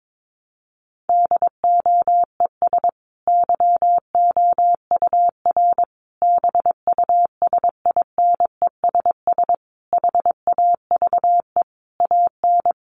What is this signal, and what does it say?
Morse code, heard as a keyed tone, about 22 wpm, 700 Hz. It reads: DOES YOUR BUSINESS HAVE AN
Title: Sentences from Top 1200 Words 22wpm